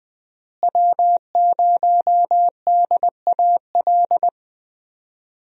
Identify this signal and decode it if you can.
Morse code, heard as a keyed tone, about 20 wpm, 700 Hz. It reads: W0DAL
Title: USA Call Signs - 5 Characters - Repeated 3x - 20wpm